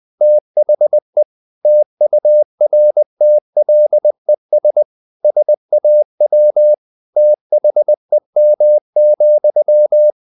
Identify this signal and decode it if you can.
Morse code, heard as a keyed tone, about 20 wpm, 600 Hz. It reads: THE TURTLES SAW THEM,
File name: the old man and the sea / the old man and the sea part 1 20wpm